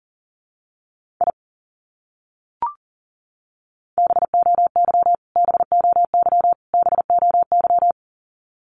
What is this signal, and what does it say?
Morse code, heard as a keyed tone, about 40 wpm, 700 Hz. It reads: I BOY BOY BOY
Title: Top 200 Words Repeated 40wpm